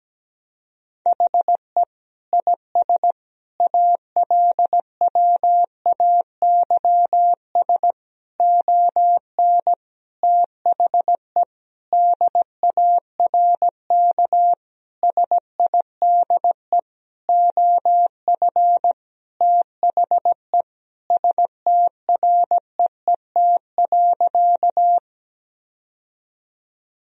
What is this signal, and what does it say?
Morse code, heard as a keyed tone, about 17 wpm, 700 Hz. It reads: HE IS ALWAYS ON THE DARK SIDE OF THE STREET.
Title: Sentences from Top 1000 Words - Review Part 3 - 17wpm